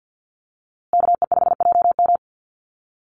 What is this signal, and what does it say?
Morse code, heard as a keyed tone, about 50 wpm, 700 Hz. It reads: KE5JM
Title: USA Call Signs Repeated 2x 50wpm